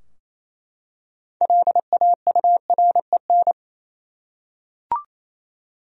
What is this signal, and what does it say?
Morse code, heard as a keyed tone, about 27 wpm, 700 Hz. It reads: LAUREN
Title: USA Common Names - 28wpm